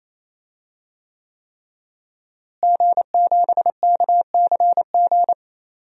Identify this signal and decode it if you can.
Morse code, heard as a keyed tone, about 28 wpm, 700 Hz. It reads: G7KCG
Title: DXCC-100 Call Signs - 5 Characters - Repeated 3x - 28wpm